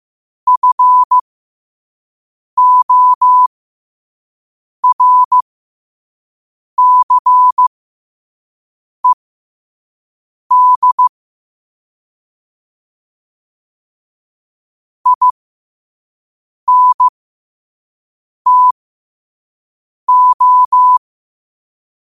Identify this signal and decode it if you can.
Morse code, heard as a keyed tone, about 15 wpm, spaced as for 5 wpm, 1000 Hz. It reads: FORCED INTO
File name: qotdfiles / 5wpm